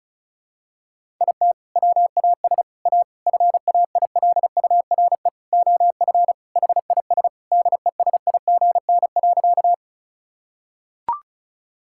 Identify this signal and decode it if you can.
Morse code, heard as a keyed tone, about 36 wpm, 700 Hz. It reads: IT WAS A FAILURE OF HIS DESIGN.
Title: Sentences from Top 1000 Words - Review Part 6 - 35wpm